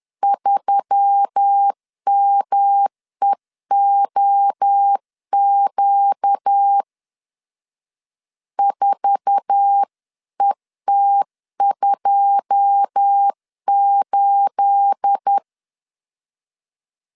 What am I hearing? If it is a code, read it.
Morse code, heard as a keyed tone, about 10 wpm, 800 Hz. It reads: 3MEOQ 4ET28
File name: LEZIONI / 9 28 10wpm 2842